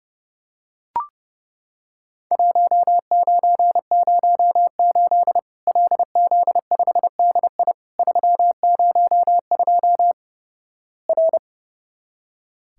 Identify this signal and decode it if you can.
Morse code, heard as a keyed tone, about 32 wpm, 700 Hz. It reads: 1908 LZ5DI 302 R
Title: Worked All Europe DX Contest - QTC Exchange - 30wpm